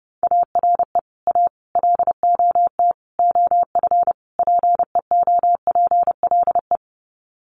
Morse code, heard as a keyed tone, about 30 wpm, 700 Hz. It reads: ARE A LOT OF PEOPLE
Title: Sets of 5 Words 30wpm